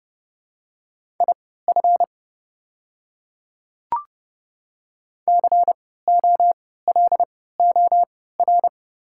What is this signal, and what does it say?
Morse code, heard as a keyed tone, about 30 wpm, spaced as for 17 wpm, 690 Hz. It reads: IF COLOR
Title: 3x Character Spacing - Top 500 Words - 30wpm